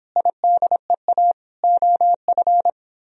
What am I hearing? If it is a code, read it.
Morse code, heard as a keyed tone, about 24 wpm, 700 Hz. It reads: IDEA OF